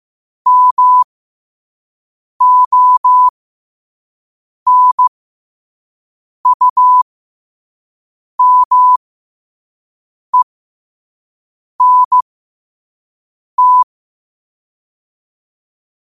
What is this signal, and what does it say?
Morse code, heard as a keyed tone, about 15 wpm, spaced as for 5 wpm, 1000 Hz. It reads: MONUMENT
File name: qotdfiles / 5wpm